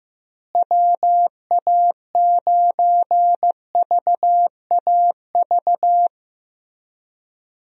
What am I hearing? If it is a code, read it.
Morse code, heard as a keyed tone, about 15 wpm, 700 Hz. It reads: WA9VAV
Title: USA Call Signs - 6 Characters - Repeated 3x - 15wpm